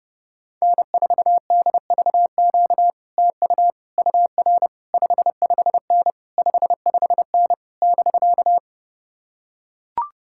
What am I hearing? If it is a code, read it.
Morse code, heard as a keyed tone, about 31 wpm, 700 Hz. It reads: N4DVQ TU UR 55N 55N <BK>